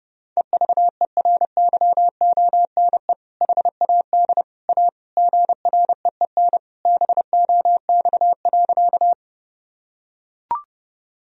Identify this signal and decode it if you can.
Morse code, heard as a keyed tone, about 32 wpm, 700 Hz. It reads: EVERYONE HAD A GREEN BOX.
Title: Sentences from Top 1000 Words - Review Part 2 - 30wpm